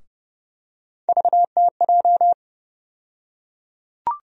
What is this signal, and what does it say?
Morse code, heard as a keyed tone, about 31 wpm, 700 Hz. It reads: VTJ
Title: USA Call Sign Postfixes - 3 Letters 30wpm